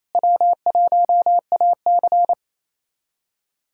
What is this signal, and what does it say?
Morse code, heard as a keyed tone, about 28 wpm, 700 Hz. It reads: W1AC
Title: USA Callsign DE Callsign - 4 Characters - 28wpm